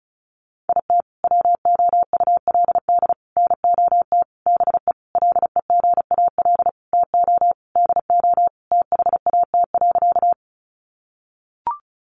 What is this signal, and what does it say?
Morse code, heard as a keyed tone, about 34 wpm, 700 Hz. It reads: IT WOULD NOT BE LEGAL TO DO THAT.
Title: Sentences from Top 1200 Words 35wpm